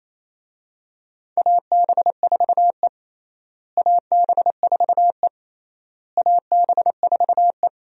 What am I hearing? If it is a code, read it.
Morse code, heard as a keyed tone, about 28 wpm, 700 Hz. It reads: AB4E AB4E AB4E
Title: USA Call Signs - 4 Characters - Repeated 3x - 28wpm